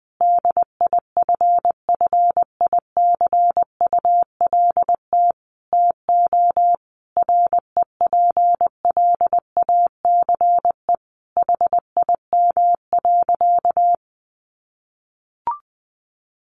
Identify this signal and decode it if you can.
Morse code, heard as a keyed tone, about 20 wpm, 700 Hz. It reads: DIFFICULT TO REPLACE HIM.
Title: Sentences from Top 1400 Words 20wpm